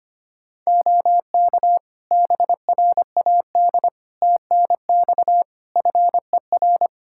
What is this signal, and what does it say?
Morse code, heard as a keyed tone, about 25 wpm, 700 Hz. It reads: OK BRAD TNX FER
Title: K1USN SST QSO Practice 25wpm